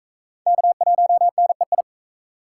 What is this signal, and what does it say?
Morse code, heard as a keyed tone, about 44 wpm, 700 Hz. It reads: K1NEI